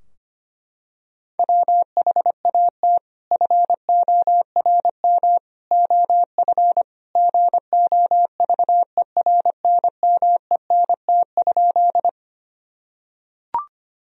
Morse code, heard as a keyed tone, about 25 wpm, 700 Hz. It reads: WHAT FORM OF GOVERNMENT?
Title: Sentences from Top 1000 Words - Review Part 3 - 25wpm